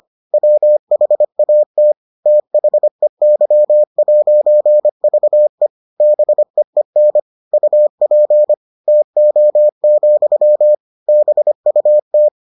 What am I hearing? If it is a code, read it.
Morse code, heard as a keyed tone, about 25 wpm, 600 Hz. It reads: WHAT THEY'VE BEEN UP TO, BUT